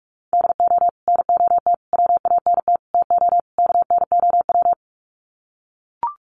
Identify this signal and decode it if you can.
Morse code, heard as a keyed tone, about 44 wpm, 700 Hz. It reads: DO NOT WANT TO KNOW
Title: Sets of 5 Words Encore 45wpm